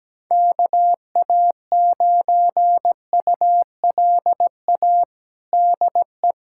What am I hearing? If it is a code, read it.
Morse code, heard as a keyed tone, about 17 wpm, 700 Hz. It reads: KA9ULA DE
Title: USA Callsign DE Callsign - 6 Characters - 17wpm